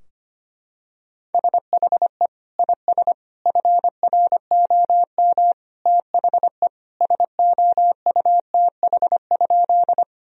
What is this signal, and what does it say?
Morse code, heard as a keyed tone, about 25 wpm, 700 Hz. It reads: SHE IS FROM THE SOUTH?
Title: Sentences from Top 1000 Words - Review Part 2 - 25wpm